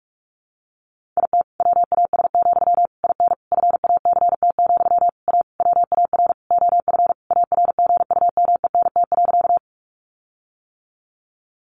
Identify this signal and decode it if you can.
Morse code, heard as a keyed tone, about 46 wpm, 700 Hz. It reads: IT WAS, IN FACT, A WAR OF ARGUMENT.
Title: Sentences from Top 1400 Words 45wpm